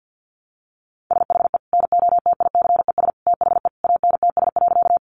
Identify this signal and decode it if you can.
Morse code, heard as a keyed tone, about 49 wpm, 700 Hz. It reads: SHE NOTICES THE ANTS.
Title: Additional Easy Phrases 50wpm